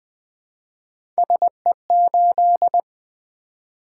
Morse code, heard as a keyed tone, about 20 wpm, 700 Hz. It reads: SE8
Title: Three Letter-Number Combinations - 20wpm